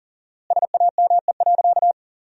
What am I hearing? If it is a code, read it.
Morse code, heard as a keyed tone, about 37 wpm, 700 Hz. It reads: SAME.
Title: 4x Word-Spacing - Sentences from Top 300 Words - 40wpm